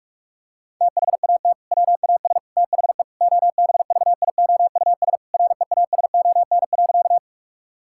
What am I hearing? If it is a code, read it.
Morse code, heard as a keyed tone, about 46 wpm, 700 Hz. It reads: THAT WAS THE OBVIOUS REASON.
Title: Sentences from Top 1600 Words 45wpm